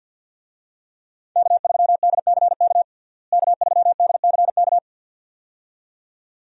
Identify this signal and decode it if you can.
Morse code, heard as a keyed tone, about 53 wpm, 700 Hz. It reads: K3DCK K3DCK
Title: USA Call Signs - 5 Characters - Repeated 2x - 50wpm